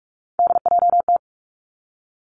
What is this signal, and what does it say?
Morse code, heard as a keyed tone, about 46 wpm, 700 Hz. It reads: DJT
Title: Three Challenging Letter-Number Combinations 45wpm